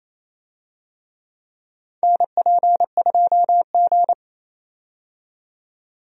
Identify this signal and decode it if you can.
Morse code, heard as a keyed tone, about 27 wpm, 700 Hz. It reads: NP2G
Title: USA Call Signs - 4 Characters - 28wpm